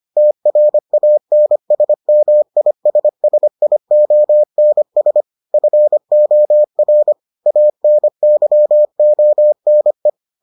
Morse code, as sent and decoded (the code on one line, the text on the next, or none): - .-. .- -. ... -- .. ... ... .. --- -. ... / ..-. --- .-. / .- -. -.-- --- -. .
TRANSMISSIONS FOR ANYONE